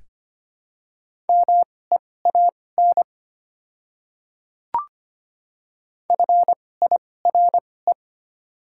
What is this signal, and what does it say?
Morse code, heard as a keyed tone, about 25 wpm, spaced as for 18 wpm, 700 Hz. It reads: MEAN FIRE
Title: 2x Character Spacing - Top 500 Words - 25wpm